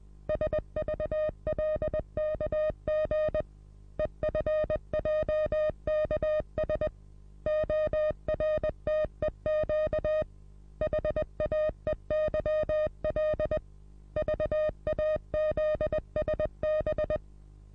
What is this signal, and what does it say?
Morse code, heard as a keyed tone, about 20 wpm, 620 Hz. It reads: SVLKG EFJKS ORTEQ HAEYL VAZSB
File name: repos / Morsekurs / DJ4UF Morsen27